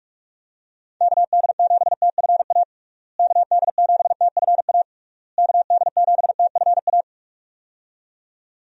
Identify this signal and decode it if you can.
Morse code, heard as a keyed tone, about 46 wpm, 700 Hz. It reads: KD7TFA KD7TFA KD7TFA
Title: USA Call Signs - 6 Characters - Repeated 3x - 45wpm